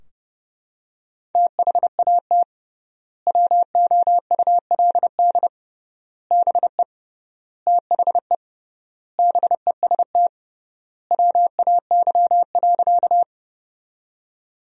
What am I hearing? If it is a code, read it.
Morse code, heard as a keyed tone, about 30 wpm, 700 Hz. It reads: THAT WOULD BE THE BEST WAY.
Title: 3x Word-Spacing - Sentences from Top 400 Words - 30wpm